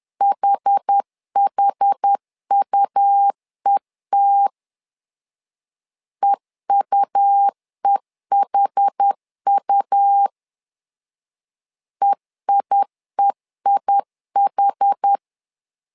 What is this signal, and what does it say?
Morse code, heard as a keyed tone, about 10 wpm, 800 Hz. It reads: HHUET EUEHU EIEIH
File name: LEZIONI / 2 UHDN / 2 UH 10wpm 1714